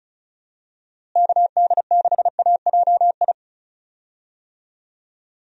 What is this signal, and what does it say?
Morse code, heard as a keyed tone, about 36 wpm, 700 Hz. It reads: KD6AJI
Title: USA Call Signs - 6 Characters - Repeated 2x - 35wpm